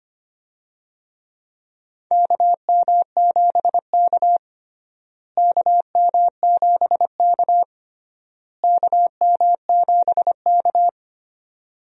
Morse code, heard as a keyed tone, about 25 wpm, 700 Hz. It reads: KM7K KM7K KM7K
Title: USA Call Signs - 4 Characters - Repeated 3x - 25wpm